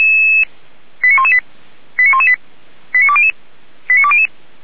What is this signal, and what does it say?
Five-tone selective call (ZVEI), heard as quick runs of five tones, tones of about 70 ms. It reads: 89118 89118 89200 89200